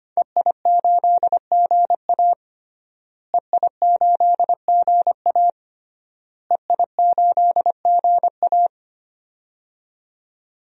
Morse code, heard as a keyed tone, about 25 wpm, 700 Hz. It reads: EI8GA EI8GA EI8GA
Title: DXCC-100 Call Signs - 5 Characters - Repeated 3x - 25wpm